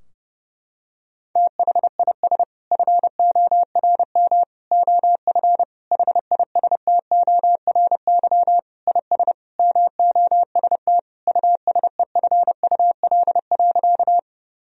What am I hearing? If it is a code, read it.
Morse code, heard as a keyed tone, about 30 wpm, 700 Hz. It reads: THIS FORM OF HISTORY IS MOST USEFUL.
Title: Sentences from Top 1000 Words - Review Part 4 - 30wpm